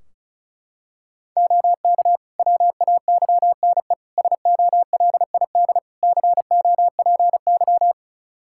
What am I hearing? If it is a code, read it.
Morse code, heard as a keyed tone, about 36 wpm, 700 Hz. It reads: OK WAYNE SOLID COPY
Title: Practice QSO - Exchange 3 - More Info - 35wpm